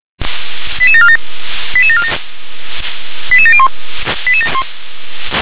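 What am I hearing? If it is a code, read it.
Five-tone selective call (ZVEI), heard as quick runs of five tones, tones of about 70 ms. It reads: 88747 88747 88701
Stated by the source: ZVEI five-tone call 88740 received by SDR.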